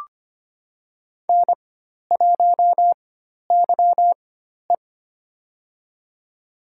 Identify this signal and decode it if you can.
Morse code, heard as a keyed tone, about 25 wpm, spaced as for 12 wpm, 700 Hz. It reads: N1YE
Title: USA Call Signs - 4 Characters - 4x Character Spacing - 25wpm